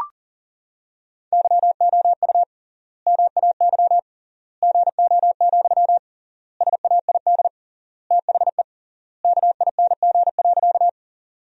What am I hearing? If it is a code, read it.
Morse code, heard as a keyed tone, about 40 wpm, 700 Hz. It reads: YOU MAY GO, SAID THE KING.
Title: 3x Word-Spacing - Sentences from Top 400 Words - 40wpm